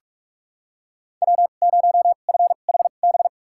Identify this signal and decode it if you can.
Morse code, heard as a keyed tone, about 44 wpm, spaced as for 32 wpm, 700 Hz. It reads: W0FHB
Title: USA Call Signs - 5 Characters - 2x Character Spacing - 45wpm